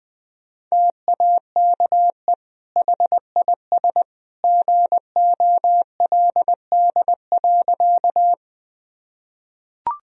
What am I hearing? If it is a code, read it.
Morse code, heard as a keyed tone, about 20 wpm, 700 Hz. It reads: TAKE HIS GOLD.
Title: Sentences from Top 1000 Words - Review Part 3 - 20wpm